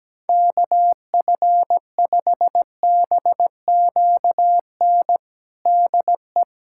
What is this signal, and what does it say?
Morse code, heard as a keyed tone, about 17 wpm, 700 Hz. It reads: KF5BQN DE
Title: USA Callsign DE Callsign - 6 Characters - 17wpm